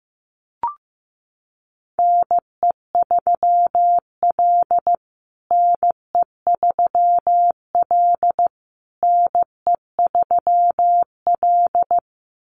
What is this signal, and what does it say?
Morse code, heard as a keyed tone, about 15 wpm, 700 Hz. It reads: NE3L NE3L NE3L
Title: USA Call Signs Repeated 3x 15wpm